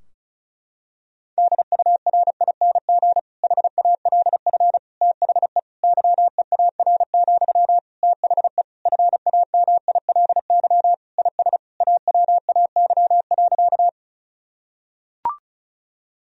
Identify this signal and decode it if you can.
Morse code, heard as a keyed tone, about 34 wpm, 700 Hz. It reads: DURING HALF THE YEAR, THE FAMILY IS AWAY.
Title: Sentences from Top 1000 Words - Review Part 2 - 35wpm